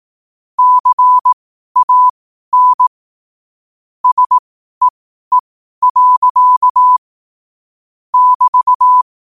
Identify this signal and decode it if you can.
Morse code, heard as a keyed tone, about 18 wpm, spaced as for 12 wpm, 1000 Hz. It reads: CAN SEE. <BT>
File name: qotdfiles / 12wpm